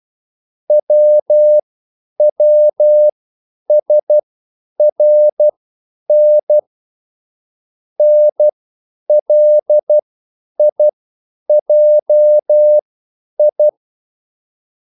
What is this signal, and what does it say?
Morse code, heard as a keyed tone, about 12 wpm, spaced as for 9 wpm, 600 Hz. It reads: WWSRN NLIJI
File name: QSL / CW lesson